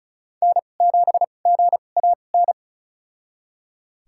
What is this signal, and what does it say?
Morse code, heard as a keyed tone, about 36 wpm, spaced as for 25 wpm, 700 Hz. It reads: N7GAN
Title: USA Call Signs - 5 Characters - 2x Character Spacing - 35wpm